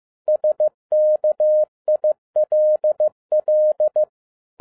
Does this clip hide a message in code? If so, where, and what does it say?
Yes: Morse code from 0.3 to 4.0 s, heard as a keyed tone, about 15 wpm, 610 Hz. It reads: SKILL